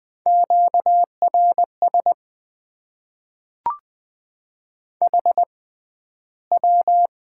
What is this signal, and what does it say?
Morse code, heard as a keyed tone, about 21 wpm, spaced as for 12 wpm, 700 Hz. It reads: QRS HW
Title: Top 90 QSO Elements - 6x Character Spacing - 20wpm